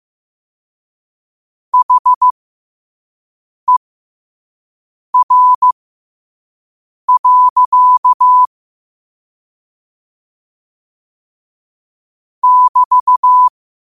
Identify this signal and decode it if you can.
Morse code, heard as a keyed tone, about 15 wpm, spaced as for 5 wpm, 1000 Hz. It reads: HER. <BT>
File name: qotdfiles / 5wpm